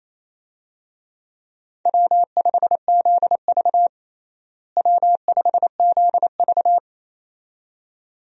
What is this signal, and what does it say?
Morse code, heard as a keyed tone, about 28 wpm, 700 Hz. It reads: W5ZV W5ZV
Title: USA Call Signs - 4 Characters - Repeated 2x - 28wpm